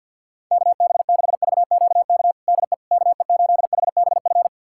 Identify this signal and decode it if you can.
Morse code, heard as a keyed tone, about 49 wpm, 700 Hz. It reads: KB6VQK DE KE8HBF
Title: USA Callsign DE Callsign - 6 Characters - 50wpm